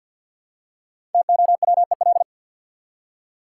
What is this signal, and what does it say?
Morse code, heard as a keyed tone, about 53 wpm, 700 Hz. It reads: TOWEL
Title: Five Letter Words - 50wpm